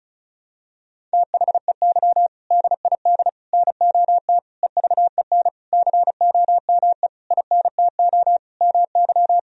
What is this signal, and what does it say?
Morse code, heard as a keyed tone, about 34 wpm, 700 Hz. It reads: THEY DID NOT EVEN COME INTO MY
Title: Sentences from Top 1000 Words - Review Part 1 - 35wpm